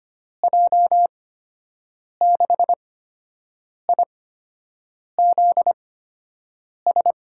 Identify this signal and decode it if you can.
Morse code, heard as a keyed tone, about 25 wpm, spaced as for 7 wpm, 700 Hz. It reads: J6IZS
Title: USA Call Signs - 6 Characters - 8x Character Spacing - 25wpm